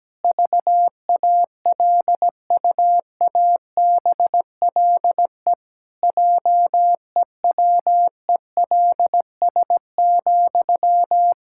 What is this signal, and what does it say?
Morse code, heard as a keyed tone, about 17 wpm, 700 Hz. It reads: VALUABLE JEWELS,
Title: Pangram Challenge v2 - 17wpm